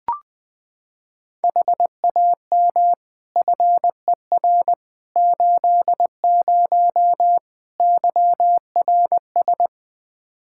Morse code, heard as a keyed tone, about 20 wpm, 700 Hz. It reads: HAM FER 80 YRS
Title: Practice QSO - Exchange 3 - More Info - 20wpm